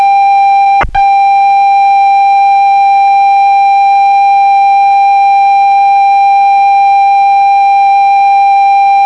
Computer data on cassette.